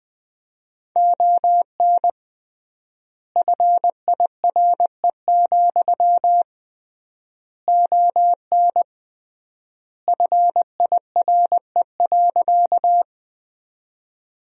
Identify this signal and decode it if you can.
Morse code, heard as a keyed tone, about 20 wpm, 700 Hz. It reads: ON FIRE, ON FIRE.
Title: 3x Word-Spacing - Sentences from Top 400 Words - 20wpm